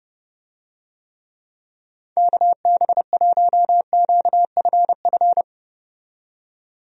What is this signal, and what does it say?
Morse code, heard as a keyed tone, about 32 wpm, 700 Hz. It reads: KB1QFF
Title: USA Call Signs - 6 Characters - 30wpm